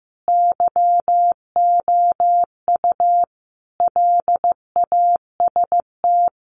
Morse code, heard as a keyed tone, about 15 wpm, 700 Hz. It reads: YOU LAST
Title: Sentences from Top 1300 Words 15wpm